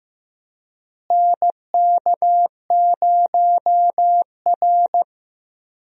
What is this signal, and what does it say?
Morse code, heard as a keyed tone, about 15 wpm, 700 Hz. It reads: NK0R
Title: USA Call Signs - 4 Characters - Repeated 2x - 15wpm